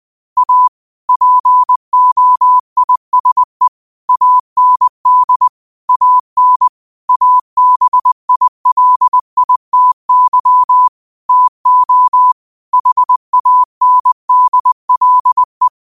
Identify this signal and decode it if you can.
Morse code, heard as a keyed tone, about 20 wpm, 1000 Hz. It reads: A POISE AND AN ABILITY TO HANDLE